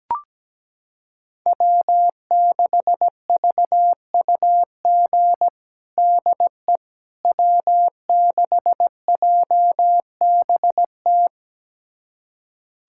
Morse code, heard as a keyed tone, about 17 wpm, 700 Hz. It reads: W6VUG DE W6JBT
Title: USA Callsign DE Callsign - 5 Characters - 17wpm